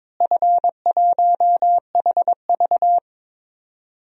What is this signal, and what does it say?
Morse code, heard as a keyed tone, about 22 wpm, 700 Hz. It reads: F1HV
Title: DXCC-100 Call Signs - 4 Characters - Repeated 3x - 22wpm